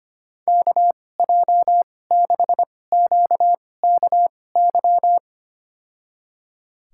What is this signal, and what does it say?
Morse code, heard as a keyed tone, about 25 wpm, spaced as for 18 wpm, 700 Hz. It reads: KJ6QKY
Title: USA Call Signs - 6 Characters - 2x Character Spacing - 25wpm